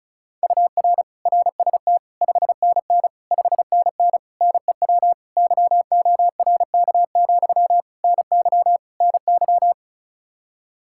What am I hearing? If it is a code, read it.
Morse code, heard as a keyed tone, about 36 wpm, 700 Hz. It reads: UR RST 5NN 5NN NEW YORK, NY NY
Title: Practice QSO - Exchange 1 - Beginning - 35wpm